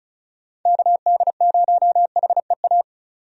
Morse code, heard as a keyed tone, about 36 wpm, 700 Hz. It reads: KD0HEA
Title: USA Call Signs - 6 Characters - 35wpm